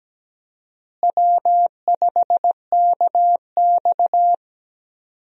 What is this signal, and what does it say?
Morse code, heard as a keyed tone, about 17 wpm, 700 Hz. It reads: W5KX
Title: USA Call Signs - 4 Characters - Repeated 3x - 17wpm